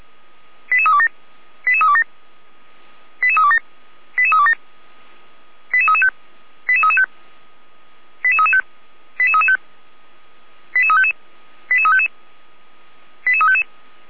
Five-tone selective call (ZVEI), heard as quick runs of five tones, tones of about 80 ms. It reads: 88327 88327 88327 88327 88335 88335 88335 88335 88355 88355 88355